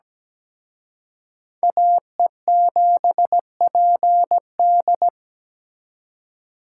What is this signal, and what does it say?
Morse code, heard as a keyed tone, about 20 wpm, 700 Hz. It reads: AE7PD
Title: Speed Racing - USA 2x2 Call Signs 17wpm